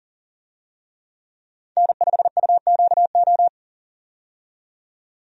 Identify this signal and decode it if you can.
Morse code, heard as a keyed tone, about 40 wpm, 700 Hz. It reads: N5UQO
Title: USA Call Signs - 5 Characters - 40wpm